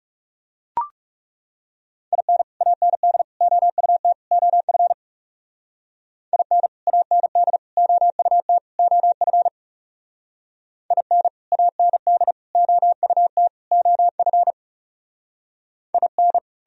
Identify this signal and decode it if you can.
Morse code, heard as a keyed tone, about 39 wpm, 700 Hz. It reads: IN AND OUT OF IN AND OUT OF IN AND OUT OF IN